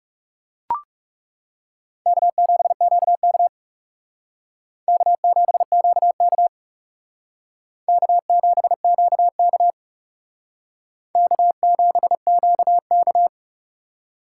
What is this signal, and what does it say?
Morse code, heard as a keyed tone, about 37 wpm, 700 Hz. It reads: K7QK K7QK K7QK K7QK
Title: Speed Racing - USA 1x2 Call Signs 30wpm